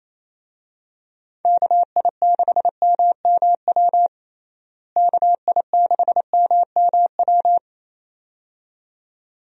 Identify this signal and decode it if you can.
Morse code, heard as a keyed tone, about 27 wpm, 700 Hz. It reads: KI6MMW KI6MMW
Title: USA Call Signs - 6 Characters - Repeated 2x - 28wpm